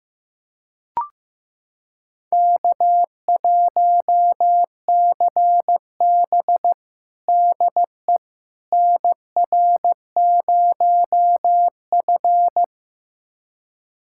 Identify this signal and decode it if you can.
Morse code, heard as a keyed tone, about 15 wpm, 700 Hz. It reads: K1CB DE NR0F
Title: USA Callsign DE Callsign - 4 Characters - 15wpm